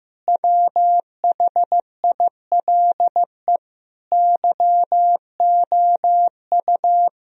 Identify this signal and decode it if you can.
Morse code, heard as a keyed tone, about 15 wpm, 700 Hz. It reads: WHILE YOU
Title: Sentences from Top 1000 Words - Review Part 1 - 15wpm